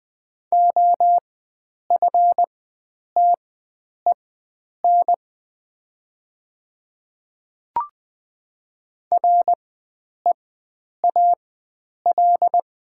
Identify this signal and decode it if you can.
Morse code, heard as a keyed tone, about 20 wpm, spaced as for 9 wpm, 700 Hz. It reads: OFTEN REAL
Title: 4x Character Spacing - Top 300 Words - 20wpm